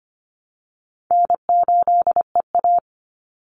Morse code, heard as a keyed tone, about 25 wpm, 700 Hz. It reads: N8EA
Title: Call Signs Repeated 3x - 25wpm